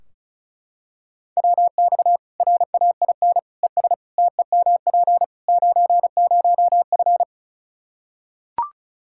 Morse code, heard as a keyed tone, about 36 wpm, 700 Hz. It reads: WX RAIN ES TEMP 90F
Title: Practice QSO - Exchange 2 - Info - 35wpm